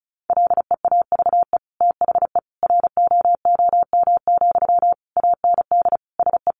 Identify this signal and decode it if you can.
Morse code, heard as a keyed tone, about 34 wpm, 700 Hz. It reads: LEAVE THE ROOM, AND SE
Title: Sentences from Top 1500 Words 35wpm